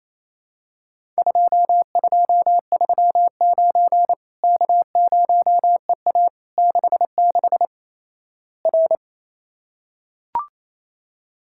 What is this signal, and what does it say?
Morse code, heard as a keyed tone, about 28 wpm, 700 Hz. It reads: 2239 K0EA 66 R
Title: Worked All Europe DX Contest - QTC Exchange - 28wpm